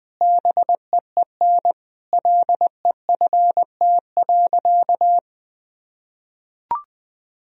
Morse code, heard as a keyed tone, about 20 wpm, 700 Hz. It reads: BEEN LEFT.
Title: Sentences from Top 1000 Words - Review Part 2 - 20wpm